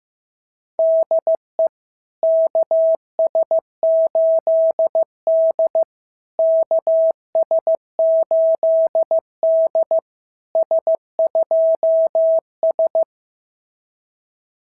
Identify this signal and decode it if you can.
Morse code, heard as a keyed tone, about 14 wpm, 650 Hz. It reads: DE KS8D KS8D S2S
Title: SOTA QSO Practice 15wpm